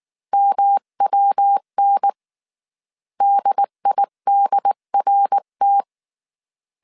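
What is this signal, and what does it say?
Morse code, heard as a keyed tone, about 19 wpm, 800 Hz. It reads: MWN DIDRT
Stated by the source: Morse code dictation at 18 wpm from the IW3SGT QRS Morse code training course, five-character groups of 300 characters.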